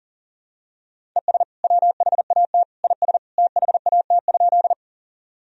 Morse code, heard as a keyed tone, about 37 wpm, 700 Hz. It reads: ES WHAT IS THAT?
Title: Sentences from Top 100 Words and Abbreviations 40wpm